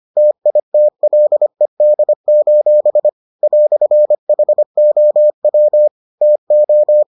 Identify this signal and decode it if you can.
Morse code, heard as a keyed tone, about 25 wpm, 600 Hz. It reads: TITLED: "HOW TO